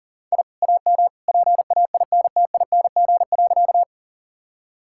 Morse code, heard as a keyed tone, about 40 wpm, 700 Hz. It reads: I AM PAINTING.
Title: Sentences from Top 1000 Words - Review Part 6 - 40wpm